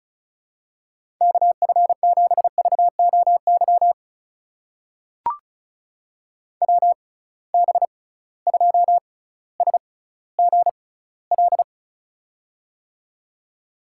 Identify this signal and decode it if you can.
Morse code, heard as a keyed tone, about 36 wpm, spaced as for 18 wpm, 700 Hz. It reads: KF7VOY WB2SGL
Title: USA Call Signs - 6 Characters - 6x Character Spacing - 35wpm